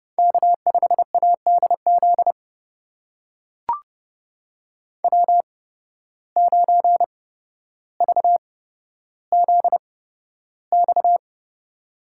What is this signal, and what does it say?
Morse code, heard as a keyed tone, about 29 wpm, spaced as for 13 wpm, 700 Hz. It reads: K5ADZ W9VZX
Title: USA Call Signs - 5 Characters - 8x Character Spacing - 30wpm